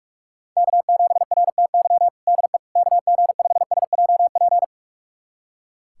Morse code, heard as a keyed tone, about 41 wpm, 700 Hz. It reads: K7RTY DE KG5SJP
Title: USA Callsign DE Callsign 45wpm